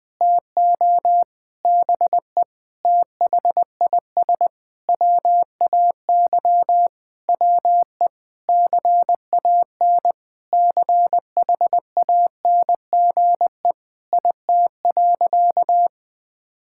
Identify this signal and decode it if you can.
Morse code, heard as a keyed tone, about 19 wpm, 700 Hz. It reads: TO BE THIS WAY WE CAN CHANGE IT.
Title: Sentences with 7-Word Phrases - 20wpm